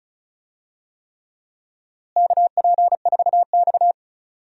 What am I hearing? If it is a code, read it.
Morse code, heard as a keyed tone, about 36 wpm, 700 Hz. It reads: KP4X
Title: DXCC-100 Call Signs - 4 Characters - Repeated 3x - 35wpm